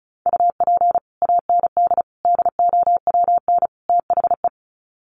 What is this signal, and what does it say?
Morse code, heard as a keyed tone, about 34 wpm, 700 Hz. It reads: UP AND DOWN THE
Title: Sets of 4 Words 35wpm